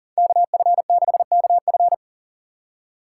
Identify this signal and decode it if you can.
Morse code, heard as a keyed tone, about 50 wpm, 700 Hz. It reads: KF6KF
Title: Speed Racing - USA 2x2 Call Signs 40wpm